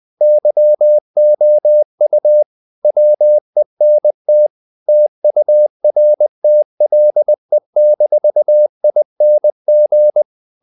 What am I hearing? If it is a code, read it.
Morse code, heard as a keyed tone, about 20 wpm, 600 Hz. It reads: YOU WENT TURTLE-ING
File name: the old man and the sea / the old man and the sea part 1 20wpm